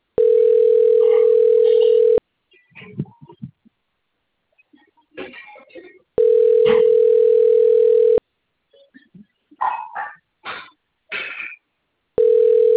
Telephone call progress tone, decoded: ringback tone

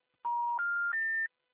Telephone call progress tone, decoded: special information tone

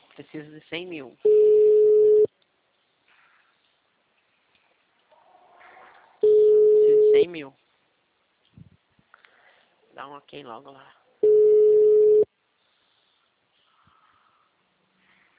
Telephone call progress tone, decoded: ringback tone

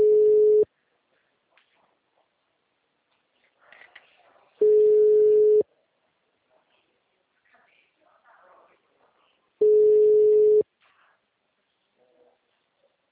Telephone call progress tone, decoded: ringback tone

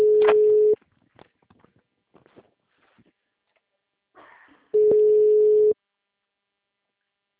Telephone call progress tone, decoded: ringback tone

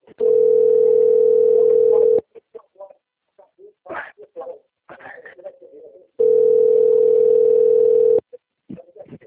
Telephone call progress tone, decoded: ringback tone

